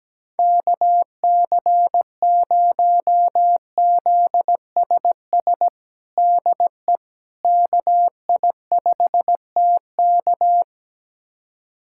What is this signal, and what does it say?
Morse code, heard as a keyed tone, about 17 wpm, 700 Hz. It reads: KC0ZSS DE KI5TK
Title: USA Callsign DE Callsign 17wpm